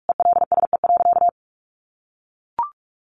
Morse code, heard as a keyed tone, about 44 wpm, 700 Hz. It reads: ELSE.
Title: Sentences from Top 1400 Words 45wpm